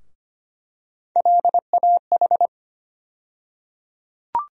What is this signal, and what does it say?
Morse code, heard as a keyed tone, about 25 wpm, 700 Hz. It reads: LAH
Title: USA Call Sign Postfixes - 3 Letters 25wpm